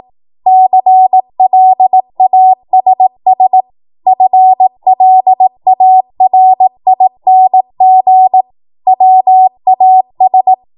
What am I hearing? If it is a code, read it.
Morse code, heard as a keyed tone, about 18 wpm, 750 Hz. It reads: CLASS FLARING WAS
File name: Morse / 18WPM